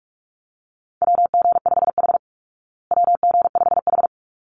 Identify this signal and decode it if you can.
Morse code, heard as a keyed tone, about 44 wpm, 700 Hz. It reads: WG5H WG5H
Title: Call Signs Repeated 3x - 45wpm